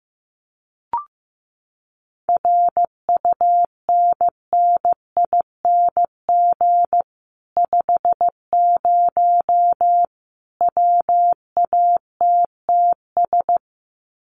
Morse code, heard as a keyed tone, about 15 wpm, 700 Hz. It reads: RUNNING 50 WATTS
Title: Common QSO Phrases 15wpm